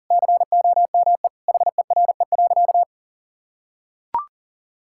Morse code, heard as a keyed tone, about 40 wpm, 700 Hz. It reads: COME HERE.
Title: Sentences from Top 1600 Words 40wpm